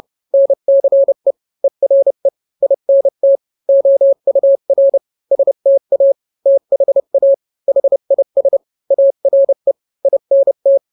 Morse code, heard as a keyed tone, about 30 wpm, 550 Hz. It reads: NCE ERE INT OUR STA THA HIS ARE INT